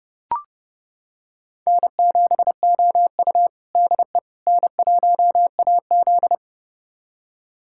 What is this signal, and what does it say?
Morse code, heard as a keyed tone, about 32 wpm, 700 Hz. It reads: N7OU DE N1AZ
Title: USA Callsign DE Callsign - 4 Characters - 30wpm